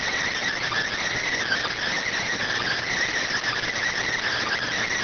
Emergency Alert System (SAME) tones, heard as harsh buzzing bursts. It is heard from 1.1 to 4.7 s.